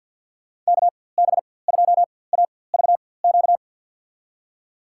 Morse code, heard as a keyed tone, about 48 wpm, spaced as for 23 wpm, 700 Hz. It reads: KB2AVQ
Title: USA Call Signs - 6 Characters - 4x Character Spacing - 50wpm